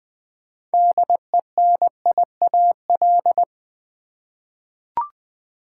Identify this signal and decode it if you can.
Morse code, heard as a keyed tone, about 20 wpm, 690 Hz. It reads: DENIAL
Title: 6 Letter Words - First 3 Letters Given - 20wpm